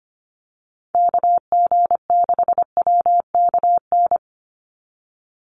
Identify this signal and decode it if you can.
Morse code, heard as a keyed tone, about 25 wpm, 700 Hz. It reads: KG6WKN
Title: USA Call Signs 25wpm